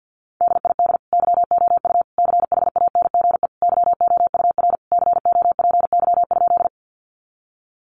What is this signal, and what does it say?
Morse code, heard as a keyed tone, about 50 wpm, 700 Hz. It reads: DID YOU CHANGE YOUR COPY?
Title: Sentences from Top 1400 Words 50wpm